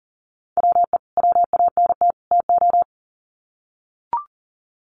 Morse code, heard as a keyed tone, about 40 wpm, 700 Hz. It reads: WE WANT TO
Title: Sets of 3 Words 40wpm